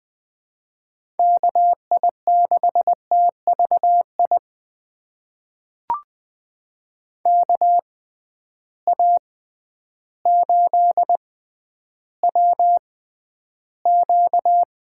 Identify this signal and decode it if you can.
Morse code, heard as a keyed tone, about 20 wpm, spaced as for 11 wpm, 700 Hz. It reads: KI6TVI KA8WQ
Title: USA Call Signs - 6 Characters - 6x Character Spacing - 20wpm